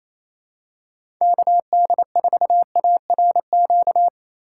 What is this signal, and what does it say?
Morse code, heard as a keyed tone, about 28 wpm, 700 Hz. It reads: KD4ARQ
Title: USA Call Signs - 6 Characters - 3x Character Spacing - 28wpm